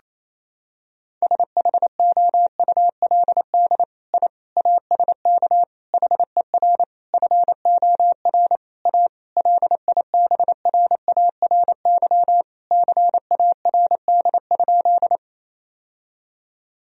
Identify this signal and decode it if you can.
Morse code, heard as a keyed tone, about 27 wpm, 700 Hz. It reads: SHOULD I ASK HER FOR A LIBRARY CARD?
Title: Sentences from Top 1000 Words - Review Part 4 - 28wpm